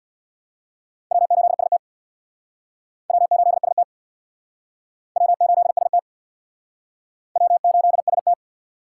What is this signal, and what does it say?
Morse code, heard as a keyed tone, about 66 wpm, 700 Hz. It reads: W8ST W8ST W8ST W8ST